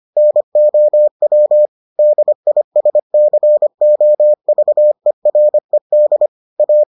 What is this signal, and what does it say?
Morse code, heard as a keyed tone, about 25 wpm, 600 Hz. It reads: NOW DISCOVERED A